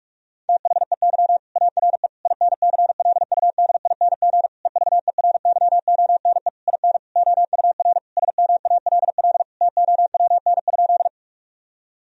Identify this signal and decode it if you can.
Morse code, heard as a keyed tone, about 46 wpm, 700 Hz. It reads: THEY ARE INCLUDING EVERYONE IN OUR SMALL TOWN?
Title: Sentences from Top 1000 Words - Review Part 4 - 45wpm